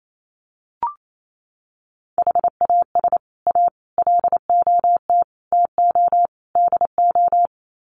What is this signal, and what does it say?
Morse code, heard as a keyed tone, about 28 wpm, 700 Hz. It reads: HAS A LOT TO DO
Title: Sets of 5 Words Encore2 28wpm